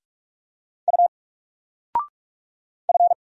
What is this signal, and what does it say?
Morse code, heard as a keyed tone, about 44 wpm, 700 Hz. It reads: U F
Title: Single Letter-Number 45wpm